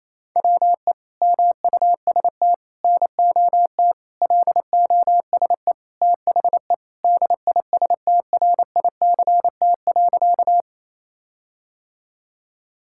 Morse code, heard as a keyed tone, about 28 wpm, 700 Hz. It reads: WE MUST NOT LOSE THE DISTRICT.
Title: Sentences from Top 1600 Words 28wpm